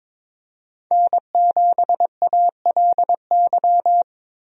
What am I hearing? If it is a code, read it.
Morse code, heard as a keyed tone, about 22 wpm, 700 Hz. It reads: N7ALY
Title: USA Call Signs - 5 Characters - Repeated 2x - 22wpm